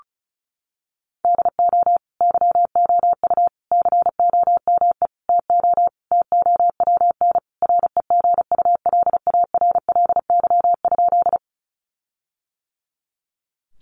Morse code, heard as a keyed tone, about 34 wpm, 700 Hz. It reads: DO YOU COME TO TOWN REGULARLY?
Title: Sentences from Top 1400 Words 35wpm